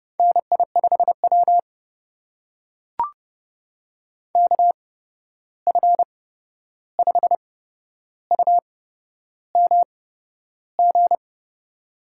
Morse code, heard as a keyed tone, about 31 wpm, spaced as for 11 wpm, 700 Hz. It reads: NI5W KF5UMG
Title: USA Call Signs - 8x Character Spacing - 30wpm